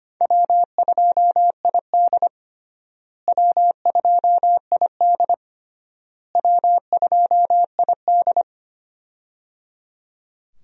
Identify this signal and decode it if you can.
Morse code, heard as a keyed tone, about 25 wpm, 700 Hz. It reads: W2ID W2ID W2ID
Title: USA Call Signs - 4 Characters - Repeated 3x - 25wpm